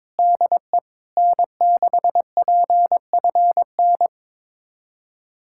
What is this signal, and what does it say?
Morse code, heard as a keyed tone, about 22 wpm, 700 Hz. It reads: DE N6PFN
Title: USA Callsign DE Callsign - 5 Characters - 22wpm